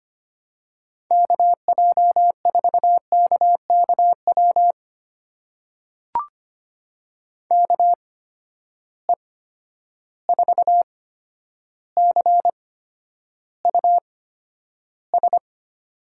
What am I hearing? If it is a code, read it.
Morse code, heard as a keyed tone, about 25 wpm, spaced as for 11 wpm, 700 Hz. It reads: KJ4KKW KE4CUS
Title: USA Call Signs - 6 Characters - 8x Character Spacing - 25wpm